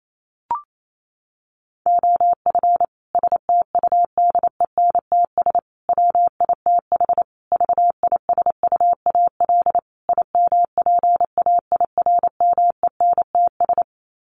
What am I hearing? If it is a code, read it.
Morse code, heard as a keyed tone, about 28 wpm, 700 Hz. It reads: OF STUDENTS WITH VISUAL IMPAIRMENTS
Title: Sets of 5 Words Encore 28wpm